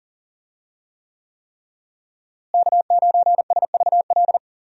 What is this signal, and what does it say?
Morse code, heard as a keyed tone, about 40 wpm, 700 Hz. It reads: K9SVL
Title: USA Call Signs - 5 Characters - Repeated 3x - 40wpm